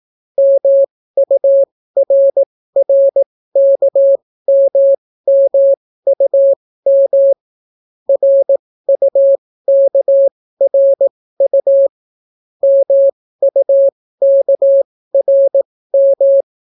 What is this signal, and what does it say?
Morse code, heard as a keyed tone, about 18 wpm, spaced as for 14 wpm, 550 Hz. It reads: MURRKMMUM RUKRU MUKRM